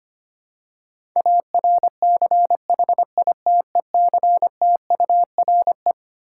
Morse code, heard as a keyed tone, about 25 wpm, 700 Hz. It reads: ARCHITECTURE
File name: Twelve Letter Words - 25wpm